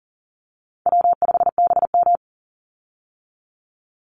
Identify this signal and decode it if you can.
Morse code, heard as a keyed tone, about 40 wpm, 700 Hz. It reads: W5BM
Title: Easy Call Signs Repeated 3x - 40wpm